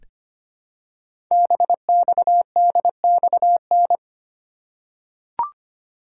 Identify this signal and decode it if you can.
Morse code, heard as a keyed tone, about 25 wpm, 700 Hz. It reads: BXDXN
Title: Code Groups - 5 Letter:Numbers - 25wpm